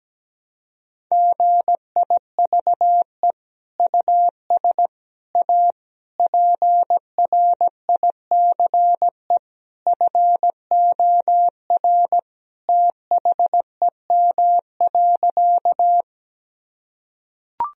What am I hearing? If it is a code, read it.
Morse code, heard as a keyed tone, about 17 wpm, 700 Hz. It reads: GIVE US A PRICE FOR THEM.
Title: Sentences from Top 1000 Words - Review Part 4 - 17wpm